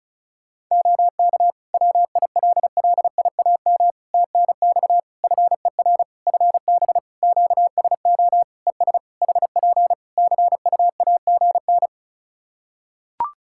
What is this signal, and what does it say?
Morse code, heard as a keyed tone, about 36 wpm, 700 Hz. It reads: OK WILLIAM TNX FER FB QSO ES HP CUAGN
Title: Practice QSO - Exchange 4 - Ending - 35wpm